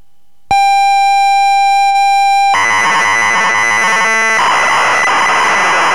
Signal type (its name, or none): computer data on cassette